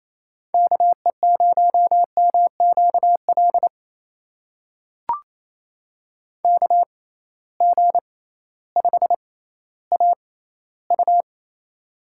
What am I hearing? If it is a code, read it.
Morse code, heard as a keyed tone, about 28 wpm, spaced as for 15 wpm, 700 Hz. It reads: KE0MQL KG5AU
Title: USA Call Signs - 6x Character Spacing - 28wpm